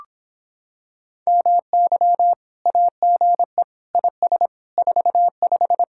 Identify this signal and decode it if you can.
Morse code, heard as a keyed tone, about 26 wpm, 700 Hz. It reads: MY AGE IS 45